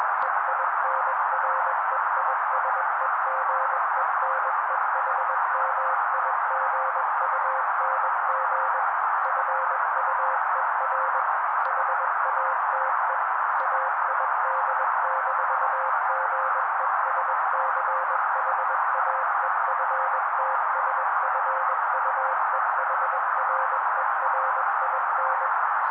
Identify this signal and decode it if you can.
Morse code, heard as a keyed tone, about 21 wpm, 530 Hz. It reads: EINREISEGENEHMIGUNG FUER HATE AID-GESCHAEFTSFUEHRERIN
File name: daily / 20wpm noise